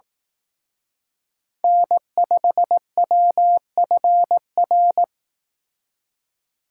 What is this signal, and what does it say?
Morse code, heard as a keyed tone, about 18 wpm, 700 Hz. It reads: N5WFR
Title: Speed Racing - USA Call Signs 15wpm